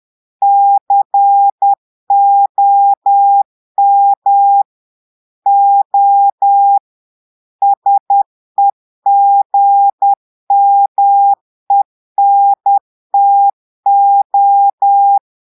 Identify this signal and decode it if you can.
Morse code, heard as a keyed tone, about 10 wpm, 800 Hz. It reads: COM O SEGMENTO
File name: portugais texte en clair 2 vitesse 10 mots minutes